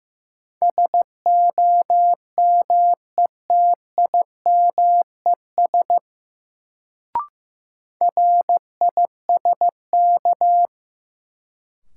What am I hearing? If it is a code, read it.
Morse code, heard as a keyed tone, about 15 wpm, 700 Hz. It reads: SOMETIMES RISK
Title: Sentences from Top 600 Words - New Word Drill - 15wpm